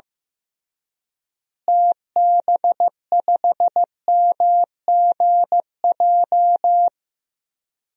Morse code, heard as a keyed tone, about 15 wpm, 700 Hz. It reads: TB5MGJ
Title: DXCC-100 Call Signs - 6 Characters - Repeated 3x - 15wpm